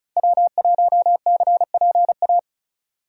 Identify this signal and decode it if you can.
Morse code, heard as a keyed tone, about 34 wpm, 700 Hz. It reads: W1CPA
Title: USA Call Signs - 5 Characters - Repeated 2x - 35wpm